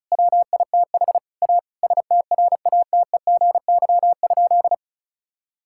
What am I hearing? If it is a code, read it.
Morse code, heard as a keyed tone, about 36 wpm, 690 Hz. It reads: WITH A STRATEGY?
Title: Sentences from Top 1000 Words - Review Part 5 - 35wpm